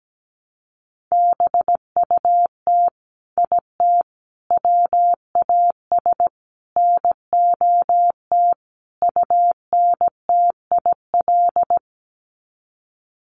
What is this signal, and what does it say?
Morse code, heard as a keyed tone, about 17 wpm, 700 Hz. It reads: BUT IT WAS NOT UNTIL
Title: Sets of 5 Words Encore2 17wpm